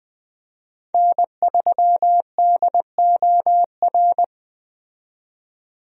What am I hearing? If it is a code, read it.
Morse code, heard as a keyed tone, about 20 wpm, 700 Hz. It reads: N3DOR
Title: USA Call Signs - 5 Characters - 20wpm